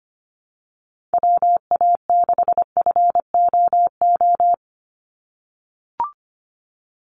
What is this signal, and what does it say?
Morse code, heard as a keyed tone, about 25 wpm, 700 Hz. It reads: WA6FOO
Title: USA Call Signs 25wpm